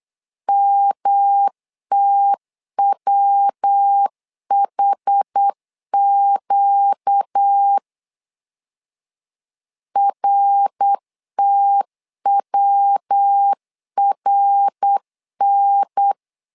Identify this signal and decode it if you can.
Morse code, heard as a keyed tone, about 8 wpm, 800 Hz. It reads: MTWHQ RTWRN
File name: 4 Q 8wpm